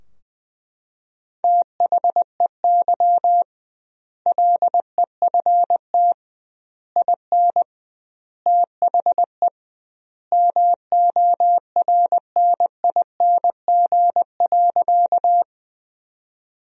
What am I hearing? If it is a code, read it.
Morse code, heard as a keyed tone, about 20 wpm, 700 Hz. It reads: THEY LEFT IN THE MORNING.
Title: 2x Word-Spacing - Sentences from Top 400 Words - 20wpm